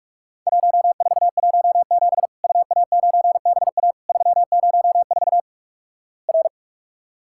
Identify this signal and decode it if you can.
Morse code, heard as a keyed tone, about 48 wpm, 700 Hz. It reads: 1417 UA9BA 304 R
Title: Worked All Europe DX Contest - QTC Exchange - 45wpm